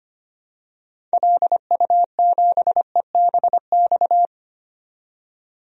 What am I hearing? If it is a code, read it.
Morse code, heard as a keyed tone, about 25 wpm, 700 Hz. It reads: LU7EBX
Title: DXCC-100 Call Signs - 6 Characters - 25wpm